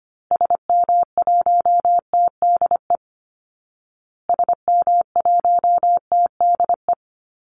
Morse code, heard as a keyed tone, about 25 wpm, 700 Hz. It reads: SM1TDE SM1TDE
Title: Call Signs Repeated 2x - 25wpm